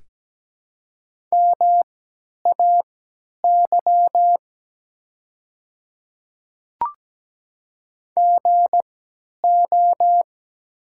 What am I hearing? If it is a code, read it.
Morse code, heard as a keyed tone, about 17 wpm, spaced as for 10 wpm, 700 Hz. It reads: MAY GO
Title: 3x Character Spacing - Top 100 Words - 17wpm